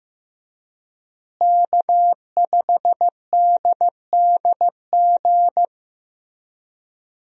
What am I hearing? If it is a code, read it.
Morse code, heard as a keyed tone, about 15 wpm, 700 Hz. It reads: K5DDG
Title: USA Call Signs - 5 Characters - Repeated 2x - 15wpm